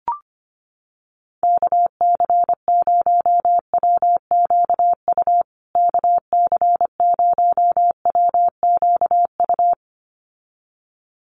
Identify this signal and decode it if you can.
Morse code, heard as a keyed tone, about 25 wpm, 700 Hz. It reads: KC0WQU KC0WQU
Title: USA Call Signs Repeated 2x 25wpm